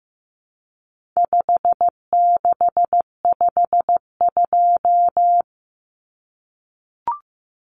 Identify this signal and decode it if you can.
Morse code, heard as a keyed tone, about 15 wpm, 700 Hz. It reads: 5652
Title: Common Cut Numbers - 4-Digit 15wpm